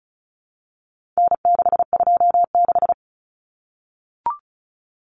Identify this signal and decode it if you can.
Morse code, heard as a keyed tone, about 34 wpm, 700 Hz. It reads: N626
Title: Common Cut Numbers - 4-Digit 35wpm